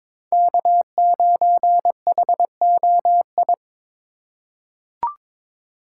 Morse code, heard as a keyed tone, about 23 wpm, 700 Hz. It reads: K9HOI
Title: USA Call Signs - 4x Character Spacing - 22wpm